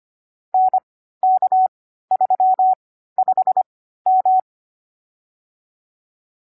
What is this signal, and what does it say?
Morse code, heard as a keyed tone, about 25 wpm, spaced as for 14 wpm, 750 Hz. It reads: NK35M
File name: CW-Koch / koch.29.3.groups5.15wpm